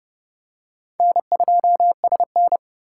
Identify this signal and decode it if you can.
Morse code, heard as a keyed tone, about 29 wpm, 700 Hz. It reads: N2SN